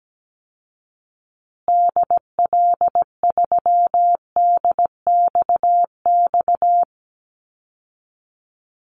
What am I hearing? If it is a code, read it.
Morse code, heard as a keyed tone, about 17 wpm, 700 Hz. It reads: DL3DXX
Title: Call Signs Repeated 3x - 17wpm